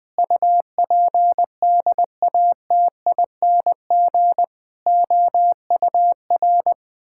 Morse code, heard as a keyed tone, about 20 wpm, 700 Hz. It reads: UPDATING OUR
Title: Sentences with 6-Word Phrases - 20wpm